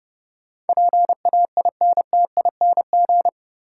Morse code, heard as a keyed tone, about 30 wpm, 700 Hz. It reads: PAINTING
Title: Sentences from Top 1000 Words - Review Part 6 - New Word Drill - 30wpm